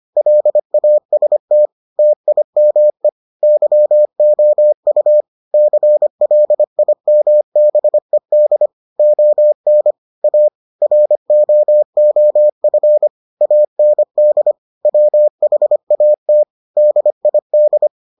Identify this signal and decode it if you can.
Morse code, heard as a keyed tone, about 25 wpm, 600 Hz. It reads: LAST TIME YOU CLIMBED ON A ROOF AND WHAT DID